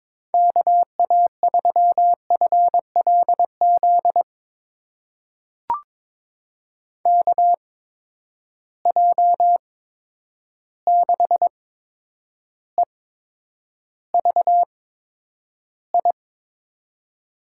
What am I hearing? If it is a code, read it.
Morse code, heard as a keyed tone, about 22 wpm, spaced as for 9 wpm, 700 Hz. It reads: KA3FLZ KJ6EVI
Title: USA Call Signs - 6 Characters - 8x Character Spacing - 22wpm